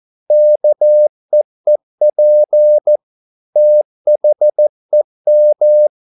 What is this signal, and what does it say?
Morse code, heard as a keyed tone, about 14 wpm, 600 Hz. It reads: KEEP THEM